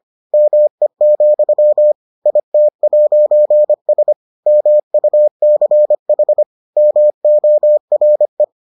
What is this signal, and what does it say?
Morse code, heard as a keyed tone, about 25 wpm, 600 Hz. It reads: ME, IT'S MUCH MORE